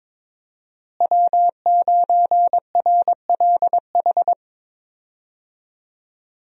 Morse code, heard as a keyed tone, about 22 wpm, 700 Hz. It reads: W9RLH
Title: USA Call Signs - 5 Characters - Repeated 2x - 22wpm